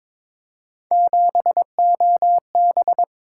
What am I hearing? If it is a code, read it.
Morse code, heard as a keyed tone, about 26 wpm, 700 Hz. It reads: 7OB
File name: Speed-Racing - Three Letter-Number Combinations - 22wpm